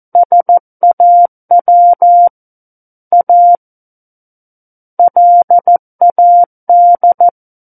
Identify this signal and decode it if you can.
Morse code, heard as a keyed tone, about 14 wpm, 700 Hz. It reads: SAW A LAD
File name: Book One / 05-Chapter 5 - 14 WPM